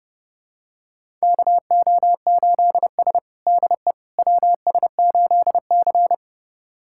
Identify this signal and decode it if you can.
Morse code, heard as a keyed tone, about 30 wpm, 700 Hz. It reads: KO8S DE WS8C
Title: USA Callsign DE Callsign - 4 Characters - 30wpm